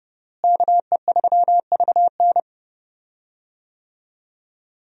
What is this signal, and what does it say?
Morse code, heard as a keyed tone, about 30 wpm, 700 Hz. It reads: KE3VN
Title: USA Call Signs - 5 Characters - Repeated 3x - 30wpm